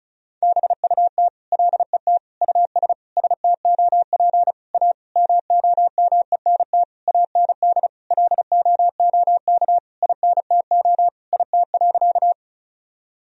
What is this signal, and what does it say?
Morse code, heard as a keyed tone, about 36 wpm, 700 Hz. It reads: BUT LET US STOP A MOMENT AND LOOK INTO IT.
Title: Sentences from Top 1000 Words - Review Part 1 - 35wpm